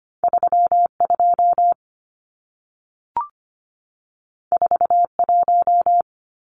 Morse code, heard as a keyed tone, about 25 wpm, 700 Hz. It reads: 32 41
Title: Numbers - 2-Digit 25wpm